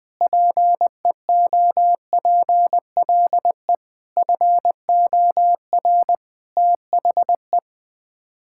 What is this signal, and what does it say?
Morse code, heard as a keyed tone, about 20 wpm, 700 Hz. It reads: PEOPLE FOR THE
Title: Sets of 8 Words - 20wpm